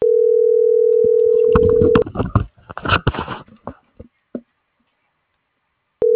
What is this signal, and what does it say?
Ringback tone (North America), 440 + 480 Hz.